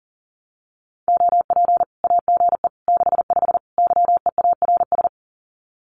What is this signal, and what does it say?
Morse code, heard as a keyed tone, about 40 wpm, 700 Hz. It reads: OP AGE 65 YEARS
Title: Common QSO Phrases 40wpm